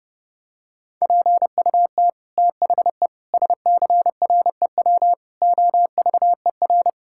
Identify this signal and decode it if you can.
Morse code, heard as a keyed tone, about 30 wpm, 700 Hz. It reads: PUT THE SCREW OVER
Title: Sentences from Top 1600 Words 30wpm